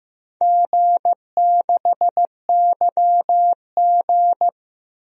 Morse code, heard as a keyed tone, about 15 wpm, 700 Hz. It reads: G6YG
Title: DXCC-100 Call Signs - 4 Characters - Repeated 2x - 15wpm